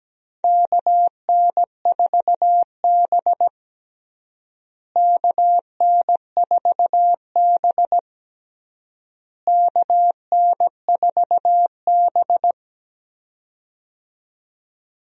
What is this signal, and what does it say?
Morse code, heard as a keyed tone, about 17 wpm, 700 Hz. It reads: KN4B KN4B KN4B
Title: USA Call Signs - 4 Characters - Repeated 3x - 17wpm